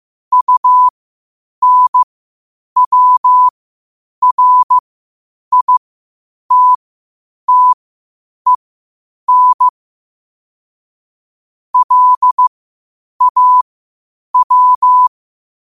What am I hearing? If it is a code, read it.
Morse code, heard as a keyed tone, about 15 wpm, spaced as for 8 wpm, 1000 Hz. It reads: UNWRITTEN LAW
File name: qotdfiles / 8wpm